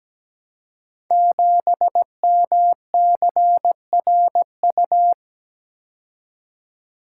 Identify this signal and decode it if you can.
Morse code, heard as a keyed tone, about 17 wpm, 700 Hz. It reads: 7MCRU
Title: Code Groups - 5 Letter:Numbers - 17wpm